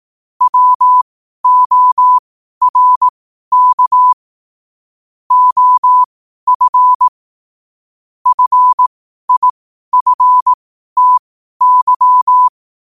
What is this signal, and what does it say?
Morse code, heard as a keyed tone, about 18 wpm, spaced as for 12 wpm, 1000 Hz. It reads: WORK OF FIFTY
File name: qotdfiles / 12wpm